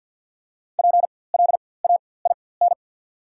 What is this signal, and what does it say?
Morse code, heard as a keyed tone, about 48 wpm, spaced as for 23 wpm, 690 Hz. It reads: PLAIN
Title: 4x Character Spacing - Top 500 Words - 50wpm